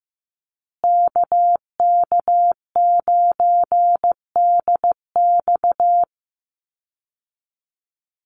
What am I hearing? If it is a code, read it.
Morse code, heard as a keyed tone, about 15 wpm, 700 Hz. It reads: KK9DX
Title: Call Signs Repeated 2x - 15wpm